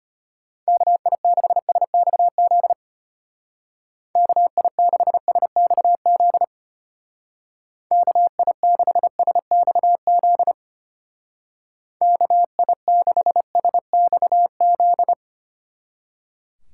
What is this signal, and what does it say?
Morse code, heard as a keyed tone, about 33 wpm, 700 Hz. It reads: KI6SXZ KI6SXZ KI6SXZ KI6SXZ